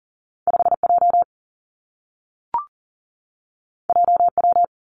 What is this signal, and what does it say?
Morse code, heard as a keyed tone, about 43 wpm, 700 Hz. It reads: 5 J J W
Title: Two Challenging Letter-Number Combinations 40wpm